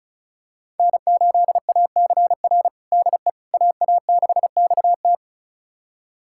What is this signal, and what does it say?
Morse code, heard as a keyed tone, about 36 wpm, 700 Hz. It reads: N8ACR DE AA6XT
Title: USA Callsign DE Callsign 35wpm